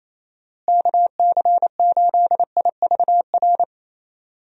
Morse code, heard as a keyed tone, about 28 wpm, 700 Hz. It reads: KC8IVR
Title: USA Call Signs - 6 Characters - Repeated 2x - 28wpm